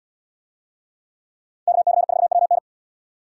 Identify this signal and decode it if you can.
Morse code, heard as a keyed tone, about 78 wpm, 700 Hz. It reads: KC4WN